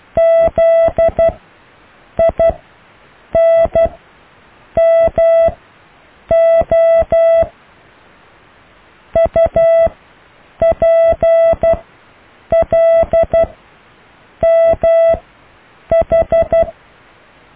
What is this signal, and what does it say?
Morse code, heard as a keyed tone, about 12 wpm, spaced as for 7 wpm, 660 Hz. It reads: ZINMO UPLMH
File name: cw / Alle letters